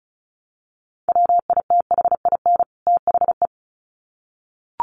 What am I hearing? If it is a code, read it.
Morse code, heard as a keyed tone, about 34 wpm, 700 Hz. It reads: WITHIN THE
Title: Sets of 2 Words 35wpm